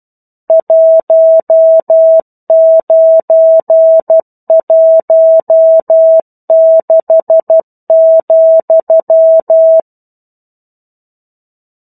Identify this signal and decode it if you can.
Morse code, heard as a keyed tone, about 12 wpm, 650 Hz. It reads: 1916,
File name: 01-Chapter-01 12 WPM